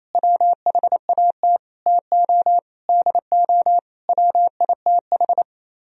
Morse code, heard as a keyed tone, about 32 wpm, 700 Hz. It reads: WHAT TO DO WITH
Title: Speed-Racing - Sets of 5 Words 28wpm